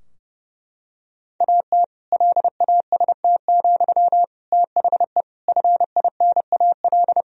Morse code, heard as a keyed tone, about 30 wpm, 700 Hz. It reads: AT LAST, THE FINAL
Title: Sentences from Top 1000 Words - Review Part 2 - 30wpm